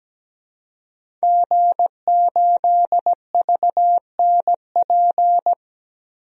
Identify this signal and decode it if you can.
Morse code, heard as a keyed tone, about 17 wpm, 700 Hz. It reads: G8VNP
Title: DXCC-100 Call Signs - 5 Characters - Repeated 3x - 17wpm